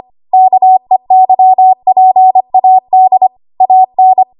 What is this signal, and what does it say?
Morse code, heard as a keyed tone, about 25 wpm, 750 Hz. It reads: KEYPAD AN
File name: Morse / 25WPM